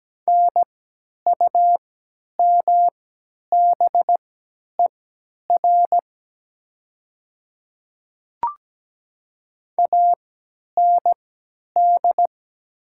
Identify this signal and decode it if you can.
Morse code, heard as a keyed tone, about 17 wpm, spaced as for 10 wpm, 700 Hz. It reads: NUMBER AND
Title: 3x Character Spacing - Top 100 Words - 17wpm